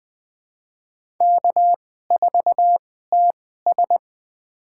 Morse code, heard as a keyed tone, about 20 wpm, spaced as for 14 wpm, 700 Hz. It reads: K4TS
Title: USA Call Signs - 4 Characters - 2x Character Spacing - 20wpm